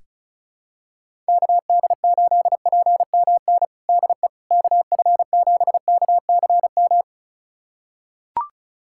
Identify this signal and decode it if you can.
Morse code, heard as a keyed tone, about 34 wpm, 700 Hz. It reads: KD8PMN DE KF7KCM
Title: USA Callsign DE Callsign - 6 Characters - 35wpm